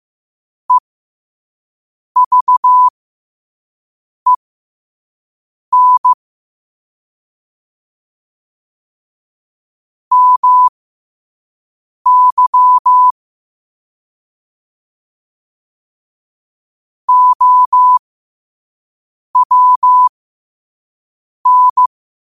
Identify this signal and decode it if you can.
Morse code, heard as a keyed tone, about 15 wpm, spaced as for 5 wpm, 1000 Hz. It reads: EVEN MY OWN
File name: qotdfiles / 5wpm